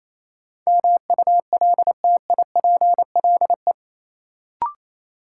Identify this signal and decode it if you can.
Morse code, heard as a keyed tone, about 28 wpm, 700 Hz. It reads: MULTIPLE
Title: Sets of 5 Words - New Word Drill - 28wpm